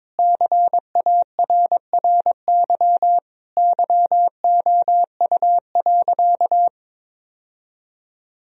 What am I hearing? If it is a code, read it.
Morse code, heard as a keyed tone, about 22 wpm, 700 Hz. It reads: CARRY YOU.
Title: Sentences from Top 1000 Words - Review Part 1 - 22wpm